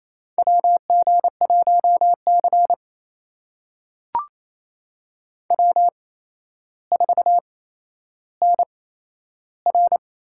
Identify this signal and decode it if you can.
Morse code, heard as a keyed tone, about 28 wpm, spaced as for 12 wpm, 700 Hz. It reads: WG1C W4NR
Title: USA Call Signs - 4 Characters - 8x Character Spacing - 28wpm